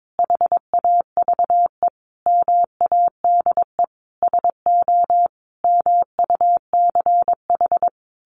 Morse code, heard as a keyed tone, about 22 wpm, 700 Hz. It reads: HAVE MADE SO MUCH
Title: Sentences from Top 1400 Words 22wpm